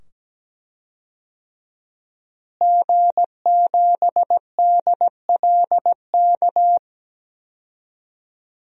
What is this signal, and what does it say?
Morse code, heard as a keyed tone, about 17 wpm, 700 Hz. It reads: G7DLK
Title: DXCC-100 Call Signs - 5 Characters - Repeated 2x - 17wpm